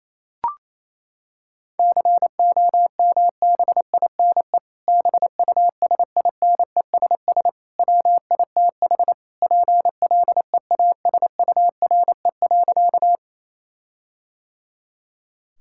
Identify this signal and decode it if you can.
Morse code, heard as a keyed tone, about 28 wpm, 700 Hz. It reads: COMBINE BUSINESS WITH PLEASURE.
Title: Sentences from Top 1600 Words 28wpm